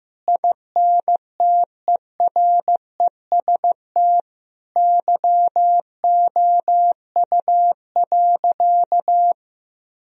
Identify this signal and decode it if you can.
Morse code, heard as a keyed tone, about 15 wpm, 700 Hz. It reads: INTEREST YOU.
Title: Sentences from Top 1000 Words - Review Part 2 - 15wpm